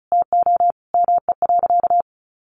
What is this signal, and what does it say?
Morse code, heard as a keyed tone, about 34 wpm, 700 Hz. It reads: TO ME.
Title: Sentences from Top 1400 Words 35wpm